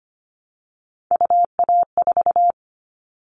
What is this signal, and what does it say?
Morse code, heard as a keyed tone, about 25 wpm, 700 Hz. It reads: UA4
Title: Callsign Prefixes for Top 25 DXCC - 25wpm